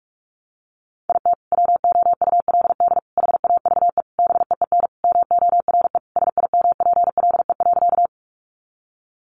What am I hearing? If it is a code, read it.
Morse code, heard as a keyed tone, about 44 wpm, 700 Hz. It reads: IT WOULD HAVE BEEN MORE SIMPLE.
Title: Sentences from Top 400 Words 45wpm